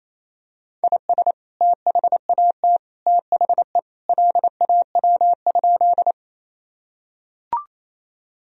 Morse code, heard as a keyed tone, about 27 wpm, 700 Hz. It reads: IS THAT THE LAW?
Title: Sentences from Top 1000 Words - Review Part 3 - 28wpm